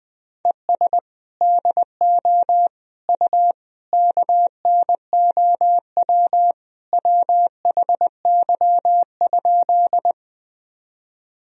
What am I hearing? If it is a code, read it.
Morse code, heard as a keyed tone, about 20 wpm, 700 Hz. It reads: ES DO U KNOW WHY?
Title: Sentences from Top 200 Words and Abbreviations 20wpm